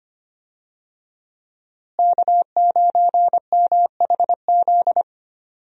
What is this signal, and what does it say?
Morse code, heard as a keyed tone, about 25 wpm, 700 Hz. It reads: K9MHZ
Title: USA Call Signs - 5 Characters - Repeated 2x - 25wpm